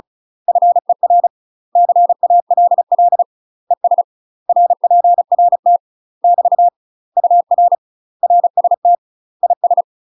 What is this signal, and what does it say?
Morse code, heard as a keyed tone, about 36 wpm, 700 Hz. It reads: FER CALL ES RPRT <BT> UR RST IS
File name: UK MASTER QSOs TRAINING/35WPM